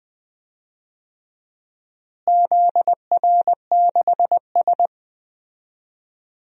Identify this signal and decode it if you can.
Morse code, heard as a keyed tone, about 20 wpm, 700 Hz. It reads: ZR6S
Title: DXCC-100 Call Signs - 4 Characters - Repeated 2x - 20wpm